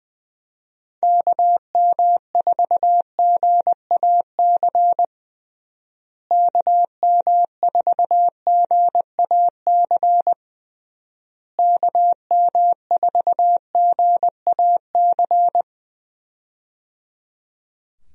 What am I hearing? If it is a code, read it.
Morse code, heard as a keyed tone, about 21 wpm, 700 Hz. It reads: KM4GAC KM4GAC KM4GAC
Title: USA Call Signs - 6 Characters - Repeated 3x - 20wpm